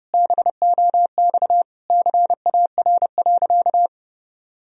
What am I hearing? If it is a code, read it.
Morse code, heard as a keyed tone, about 30 wpm, 700 Hz. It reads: BOX CAR.
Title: Sentences from Top 1000 Words - Review Part 2 - 30wpm